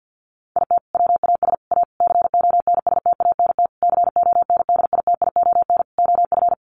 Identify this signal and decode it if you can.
Morse code, heard as a keyed tone, about 50 wpm, 700 Hz. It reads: IT WAS A CONSTANT CONDITION OF
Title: Sentences from Top 1500 Words 50wpm